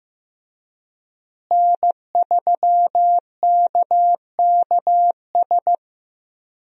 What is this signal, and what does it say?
Morse code, heard as a keyed tone, about 15 wpm, 700 Hz. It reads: N3KKS
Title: USA Call Signs - 5 Characters - Repeated 3x - 15wpm